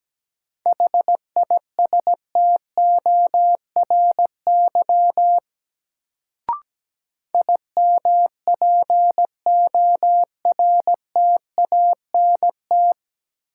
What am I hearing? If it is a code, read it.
Morse code, heard as a keyed tone, about 17 wpm, 700 Hz. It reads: HISTORY IMPORTANT
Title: Sentences from Top 600 Words - New Word Drill - 17wpm